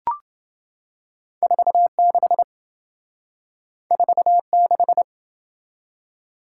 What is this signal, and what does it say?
Morse code, heard as a keyed tone, about 27 wpm, 700 Hz. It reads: 46 46